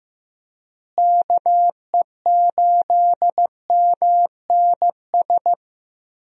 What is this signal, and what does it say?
Morse code, heard as a keyed tone, about 15 wpm, 700 Hz. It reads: KE8MNS
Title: USA Call Signs - 6 Characters - Repeated 3x - 15wpm